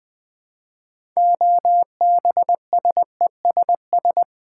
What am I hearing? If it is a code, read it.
Morse code, heard as a keyed tone, about 20 wpm, 700 Hz. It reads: OBSESS
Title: Build A Word - 3 Letters At A Time - 6-Letter Words - 20wpm